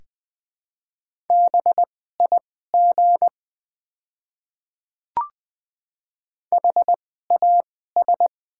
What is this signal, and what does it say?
Morse code, heard as a keyed tone, about 20 wpm, spaced as for 14 wpm, 700 Hz. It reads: BIG HAS
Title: 2x Character Spacing - Top 300 Words - 20wpm